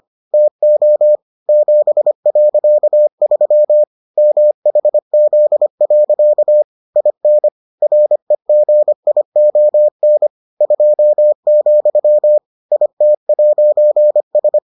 Morse code, heard as a keyed tone, about 25 wpm, 600 Hz. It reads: TO 7.3 MHZ. IN REGION 2, IT'S